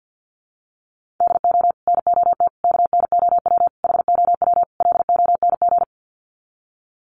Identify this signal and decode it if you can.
Morse code, heard as a keyed tone, about 50 wpm, 700 Hz. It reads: DO NOT KNOW HOW LONG
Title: Sets of 5 Words Encore2 50wpm